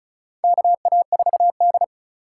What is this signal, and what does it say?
Morse code, heard as a keyed tone, about 36 wpm, 700 Hz. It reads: KA4D
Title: USA Call Signs - 4 Characters - 3x Character Spacing - 35wpm